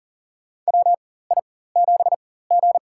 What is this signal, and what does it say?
Morse code, heard as a keyed tone, about 40 wpm, spaced as for 19 wpm, 700 Hz. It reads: WI7G
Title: USA Call Signs - 4 Characters - 4x Character Spacing - 40wpm